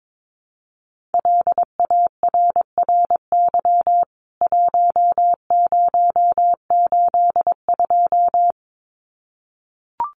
Morse code, heard as a keyed tone, about 22 wpm, 700 Hz. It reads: LARRY 1082
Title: CWT Exchange 22wpm